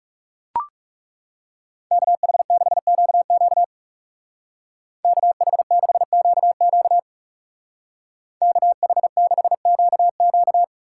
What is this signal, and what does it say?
Morse code, heard as a keyed tone, about 36 wpm, 700 Hz. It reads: KH6QQ KH6QQ KH6QQ
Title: Speed Racing - USA 2x2 Call Signs 30wpm